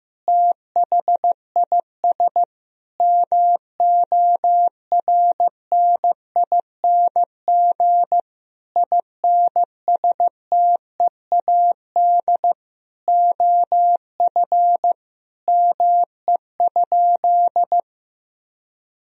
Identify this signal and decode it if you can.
Morse code, heard as a keyed tone, about 15 wpm, 700 Hz. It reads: THIS MORNING INSTEAD OF ME?
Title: Sentences from Top 1000 Words - Review Part 4 - 15wpm